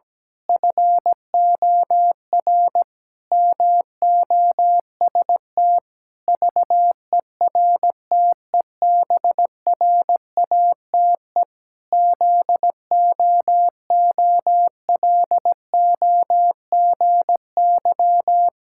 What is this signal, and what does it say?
Morse code, heard as a keyed tone, about 17 wpm, 700 Hz. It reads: FOR MOST VERTEBRATE ZOOLOGY
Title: Pangram Challenge v2 - 17wpm